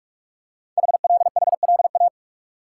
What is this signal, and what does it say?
Morse code, heard as a keyed tone, about 44 wpm, 700 Hz. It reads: HLHLA
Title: Code Groups - 5 Letters - 45wpm